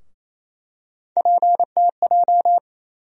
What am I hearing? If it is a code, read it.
Morse code, heard as a keyed tone, about 27 wpm, 700 Hz. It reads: PTJ
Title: USA Call Sign Postfixes - 3 Letters 28wpm